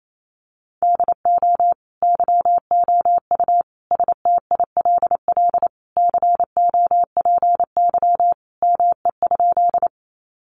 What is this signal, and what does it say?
Morse code, heard as a keyed tone, about 28 wpm, 700 Hz. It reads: DO YOU STILL COPY ME?
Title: Sentences from Top 1400 Words 28wpm